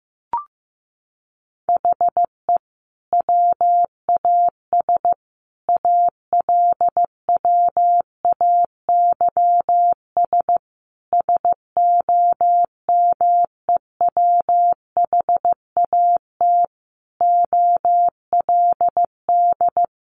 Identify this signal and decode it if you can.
Morse code, heard as a keyed tone, about 15 wpm, 700 Hz. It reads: HE WAS ALWAYS SOMEWHAT OLD
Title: Sentences from Top 1300 Words 15wpm